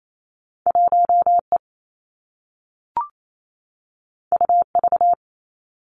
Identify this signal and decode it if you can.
Morse code, heard as a keyed tone, about 28 wpm, 700 Hz. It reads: 1 E U V
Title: Two Challenging Letter-Number Combinations 28wpm